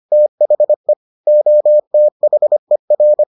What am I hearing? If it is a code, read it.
Morse code, heard as a keyed tone, about 25 wpm, 600 Hz. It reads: THE OTHER